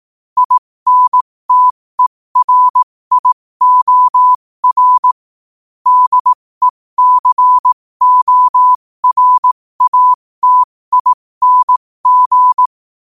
Morse code, heard as a keyed tone, about 18 wpm, spaced as for 15 wpm, 1000 Hz. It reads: INTERIOR DECORATING
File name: qotdfiles / 15wpm